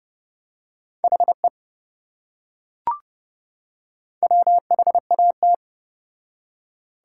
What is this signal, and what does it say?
Morse code, heard as a keyed tone, about 27 wpm, 700 Hz. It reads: HE WHAT